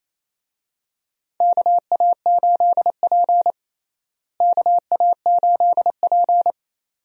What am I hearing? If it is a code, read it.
Morse code, heard as a keyed tone, about 28 wpm, 700 Hz. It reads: KA8P KA8P
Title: USA Call Signs - 4 Characters - Repeated 2x - 28wpm